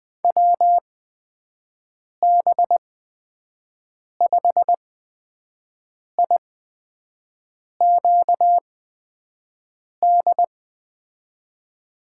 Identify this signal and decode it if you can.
Morse code, heard as a keyed tone, about 20 wpm, spaced as for 5 wpm, 700 Hz. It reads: WB5IQD
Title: USA Call Signs - 8x Character Spacing - 20wpm